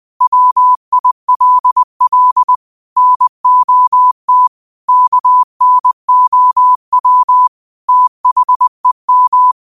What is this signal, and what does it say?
Morse code, heard as a keyed tone, about 20 wpm, 1000 Hz. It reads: WILL NOT KNOW THEM
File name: qotdfiles / 20wpm